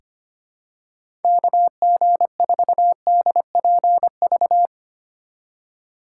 Morse code, heard as a keyed tone, about 25 wpm, 700 Hz. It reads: KG4DPV
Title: USA Call Signs - 6 Characters - 2x Character Spacing - 25wpm